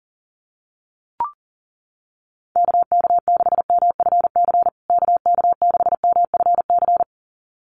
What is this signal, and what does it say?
Morse code, heard as a keyed tone, about 40 wpm, 700 Hz. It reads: KK6MFC KK6MFC
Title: USA Call Signs Repeated 2x 40wpm